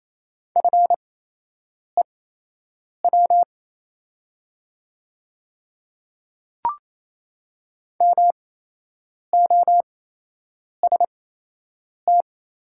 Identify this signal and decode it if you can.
Morse code, heard as a keyed tone, about 28 wpm, spaced as for 8 wpm, 700 Hz. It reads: FEW MOST
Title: 8x Character Spacing - Top 300 Words - 28wpm